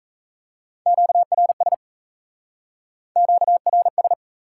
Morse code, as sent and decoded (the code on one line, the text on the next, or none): --.- .-. ... / --.- .-. ...
QRS QRS